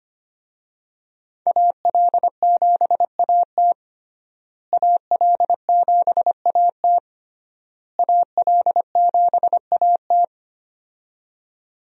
Morse code, heard as a keyed tone, about 25 wpm, 700 Hz. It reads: AL7AT AL7AT AL7AT
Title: USA Call Signs - 5 Characters - Repeated 3x - 25wpm